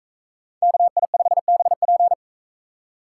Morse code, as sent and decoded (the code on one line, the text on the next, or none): -.- .. ..... -... .--.
KI5BP